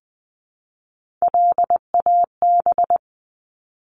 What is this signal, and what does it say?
Morse code, heard as a keyed tone, about 20 wpm, 700 Hz. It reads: LAB
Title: Top 1751-2000 Words 20wpm